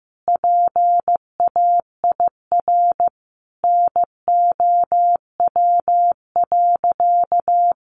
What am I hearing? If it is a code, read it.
Morse code, heard as a keyed tone, about 15 wpm, 700 Hz. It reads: PAIR NOW.
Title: Sentences from Top 1200 Words 15wpm